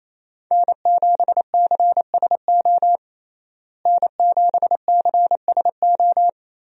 Morse code, heard as a keyed tone, about 28 wpm, 700 Hz. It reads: N7CSO N7CSO
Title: USA Call Signs - 5 Characters - Repeated 3x - 28wpm